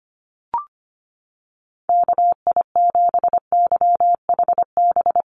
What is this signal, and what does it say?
Morse code, heard as a keyed tone, about 25 wpm, 700 Hz. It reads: KI7YHB
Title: USA Call Signs Repeated 3x 25wpm